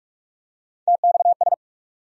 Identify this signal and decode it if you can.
Morse code, heard as a keyed tone, about 47 wpm, 700 Hz. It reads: TXS
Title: USA Call Sign Postfixes - 3 Letters 45wpm